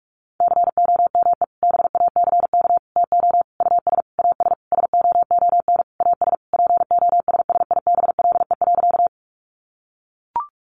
Morse code, heard as a keyed tone, about 44 wpm, 700 Hz. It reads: COME BACK TO US AS SOON AS POSSIBLE.
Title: Sentences from Top 500 Words 45wpm